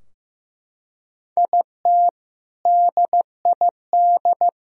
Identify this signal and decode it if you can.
Morse code, heard as a keyed tone, about 15 wpm, 700 Hz. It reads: IT DID
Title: Sentences from Top 1000 Words - Review Part 2 - 15wpm